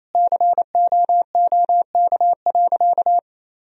Morse code, heard as a keyed tone, about 28 wpm, 700 Hz. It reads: COOK.
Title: Sentences from Top 1000 Words - Review Part 4 - 28wpm